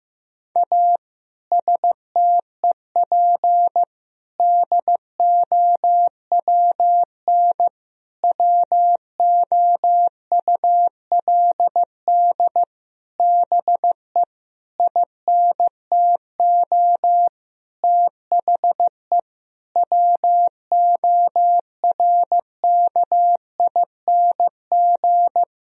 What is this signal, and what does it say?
Morse code, heard as a keyed tone, about 15 wpm, 700 Hz. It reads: A STEP DOWN WOULD BE INTO THE WORKING
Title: Sentences from Top 1000 Words - Review Part 4 - 15wpm